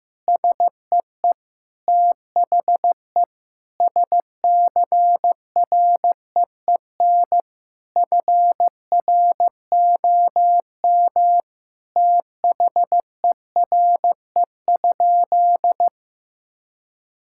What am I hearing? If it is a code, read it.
Morse code, heard as a keyed tone, about 15 wpm, 700 Hz. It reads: SEE THE SCREEN FROM THERE?
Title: Sentences from Top 1000 Words - Review Part 6 - 15wpm